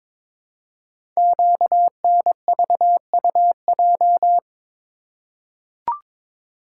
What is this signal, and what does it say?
Morse code, heard as a keyed tone, about 22 wpm, 700 Hz. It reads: QNVUJ
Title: Code Groups - 5 Letters - 22wpm